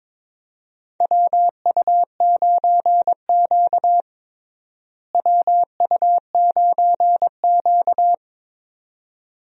Morse code, heard as a keyed tone, about 22 wpm, 700 Hz. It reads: WU9Q WU9Q
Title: USA Call Signs - 4 Characters - Repeated 2x - 22wpm